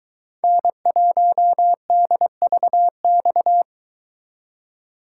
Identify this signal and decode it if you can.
Morse code, heard as a keyed tone, about 23 wpm, 700 Hz. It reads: N1DVX